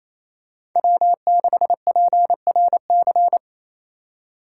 Morse code, heard as a keyed tone, about 28 wpm, 700 Hz. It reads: W6PRC
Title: USA Call Signs - 5 Characters - 4x Character Spacing - 28wpm